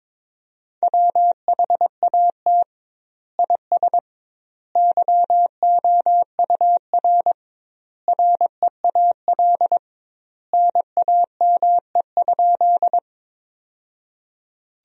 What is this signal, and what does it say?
Morse code, heard as a keyed tone, about 22 wpm, 700 Hz. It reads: WHAT IS YOUR REAL NAME?
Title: 2x Word-Spacing - Sentences from Top 300 Words - 22wpm